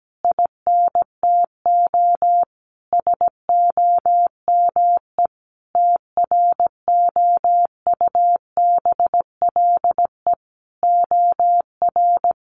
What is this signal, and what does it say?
Morse code, heard as a keyed tone, about 17 wpm, 700 Hz. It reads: INTO SOME TROUBLE OR
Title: Sentences from Top 1500 Words 17wpm